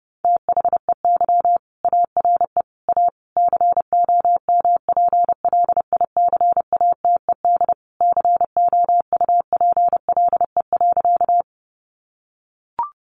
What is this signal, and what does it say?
Morse code, heard as a keyed tone, about 30 wpm, 700 Hz. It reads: THEY ARE A COMPLICATED COUPLE.
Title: Sentences from Top 1200 Words 30wpm